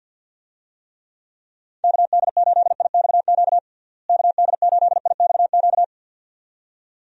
Morse code, heard as a keyed tone, about 51 wpm, 700 Hz. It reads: KD8IXQ KD8IXQ
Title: USA Call Signs - 6 Characters - Repeated 2x - 50wpm